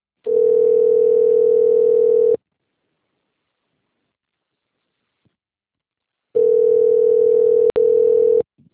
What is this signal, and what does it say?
Ringback tone (North America), 440 + 480 Hz.